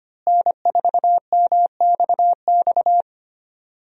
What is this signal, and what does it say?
Morse code, heard as a keyed tone, about 25 wpm, 700 Hz. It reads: N4MXX
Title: USA Call Signs - 5 Characters - Repeated 3x - 25wpm